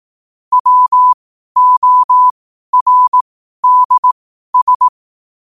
Morse code, heard as a keyed tone, about 18 wpm, spaced as for 12 wpm, 1000 Hz. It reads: WORDS
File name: qotdfiles / 12wpm